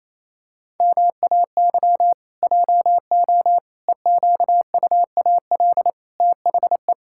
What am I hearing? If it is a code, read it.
Morse code, heard as a keyed tone, about 28 wpm, 700 Hz. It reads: MAY JO EQUAL THE
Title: Pangram Challenge v2 - 28wpm